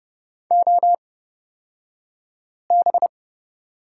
Morse code, heard as a keyed tone, about 29 wpm, spaced as for 5 wpm, 690 Hz. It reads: OB
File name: Lesson 36d - BT - Two Characters 30wpm